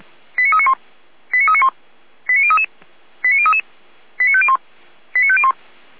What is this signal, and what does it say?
Five-tone selective call (ZVEI), heard as quick runs of five tones, tones of about 70 ms. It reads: 89391 89391 89033 89033 89501 89501